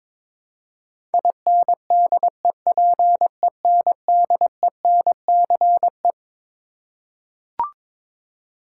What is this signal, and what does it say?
Morse code, heard as a keyed tone, about 22 wpm, 700 Hz. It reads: INDEPENDENCE
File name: Twelve Letter Words - 22wpm